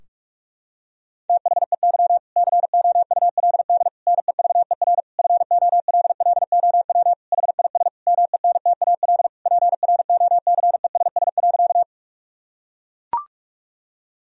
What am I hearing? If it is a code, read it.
Morse code, heard as a keyed tone, about 41 wpm, 700 Hz. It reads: THEY COULD NEVER FOLLOW HIS MENTAL PROCESS.
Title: Sentences from Top 1000 Words - Review Part 4 - 45wpm